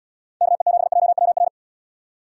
Morse code, heard as a keyed tone, about 78 wpm, 700 Hz. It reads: KE7JWR